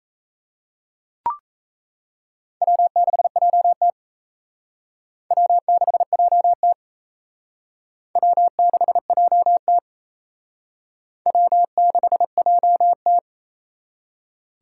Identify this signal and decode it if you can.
Morse code, heard as a keyed tone, about 37 wpm, 700 Hz. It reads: W6JT W6JT W6JT W6JT